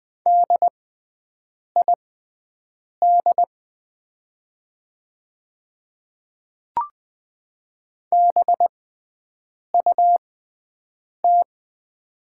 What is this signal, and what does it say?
Morse code, heard as a keyed tone, about 20 wpm, spaced as for 7 wpm, 700 Hz. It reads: DID BUT
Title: 6x Character Spacing - Top 100 Words - 20wpm